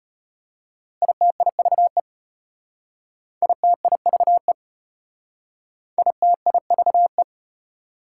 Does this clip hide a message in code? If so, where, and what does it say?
Yes: Morse code from 1.0 to 7.2 s, heard as a keyed tone, about 35 wpm, 700 Hz. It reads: ITIVE ITIVE ITIVE